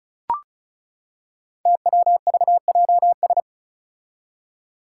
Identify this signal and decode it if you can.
Morse code, heard as a keyed tone, about 36 wpm, 700 Hz. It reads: TWVJS
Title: Five Challenging Letter-Number Combinations 35wpm